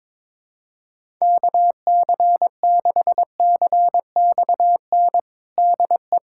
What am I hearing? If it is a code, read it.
Morse code, heard as a keyed tone, about 21 wpm, 700 Hz. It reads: KC6CXN DE
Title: USA Callsign DE Callsign - 6 Characters - 22wpm